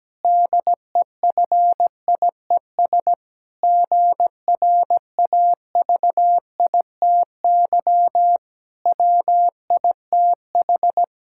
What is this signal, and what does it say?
Morse code, heard as a keyed tone, about 17 wpm, 700 Hz. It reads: DEFIES GRAVITY WITH
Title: Pangram Challenge v2 - 17wpm